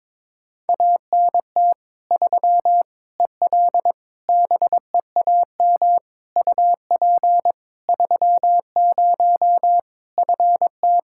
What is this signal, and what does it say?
Morse code, heard as a keyed tone, about 22 wpm, 700 Hz. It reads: ANT 3 EL BEAM UP 30 FT
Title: Practice QSO - Exchange 2 - Info - 22wpm